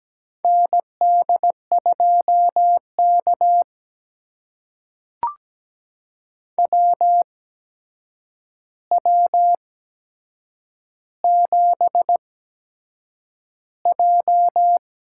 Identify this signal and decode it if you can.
Morse code, heard as a keyed tone, about 17 wpm, spaced as for 7 wpm, 700 Hz. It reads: ND2K WW7J
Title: USA Call Signs - 8x Character Spacing - 17wpm